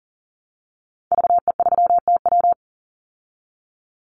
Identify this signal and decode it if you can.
Morse code, heard as a keyed tone, about 40 wpm, 700 Hz. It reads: VE3TW
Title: Call Signs Repeated 3x - 40wpm